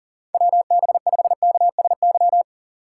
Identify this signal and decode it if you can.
Morse code, heard as a keyed tone, about 40 wpm, 700 Hz. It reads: WB5KSY
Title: USA Call Signs - 6 Characters - 6x Character Spacing - 40wpm